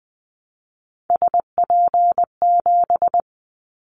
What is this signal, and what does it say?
Morse code, heard as a keyed tone, about 20 wpm, 700 Hz. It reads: SP7
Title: Callsign Prefixes for Top 25 DXCC - 20wpm